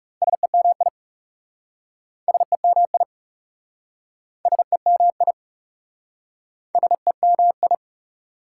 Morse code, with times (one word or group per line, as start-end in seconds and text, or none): 0.2-0.9 SEMI
2.3-3.0 SEMI
4.5-5.3 SEMI
6.8-7.7 SEMI